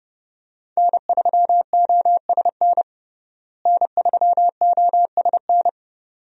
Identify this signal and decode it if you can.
Morse code, heard as a keyed tone, about 30 wpm, 700 Hz. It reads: N3OSN N3OSN
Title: USA Call Signs - 5 Characters - Repeated 2x - 30wpm